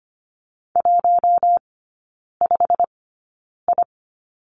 Morse code, heard as a keyed tone, about 25 wpm, spaced as for 9 wpm, 700 Hz. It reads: 15I
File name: course / 25:10wpm / 093 Lesson 16e - Letter c - Three Characters 25wpm